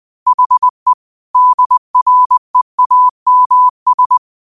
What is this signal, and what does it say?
Morse code, heard as a keyed tone, about 20 wpm, 1000 Hz. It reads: HE DREAMS
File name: qotdfiles / 20wpm